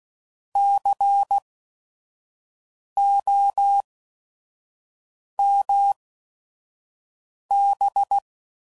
Morse code, heard as a keyed tone, about 16 wpm, spaced as for 5 wpm, 800 Hz. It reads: COMB
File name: Prefixes / By First Letter / Cs